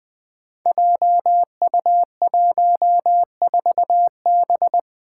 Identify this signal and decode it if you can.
Morse code, heard as a keyed tone, about 20 wpm, 700 Hz. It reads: JU14B
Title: Five Challenging Letter-Number Combinations 20wpm